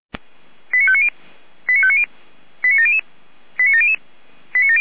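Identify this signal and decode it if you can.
Five-tone selective call (ZVEI), heard as quick runs of five tones, tones of about 70 ms. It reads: 89590 89590 89700 89700
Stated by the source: ZVEI five-tone selective call received with an SDR.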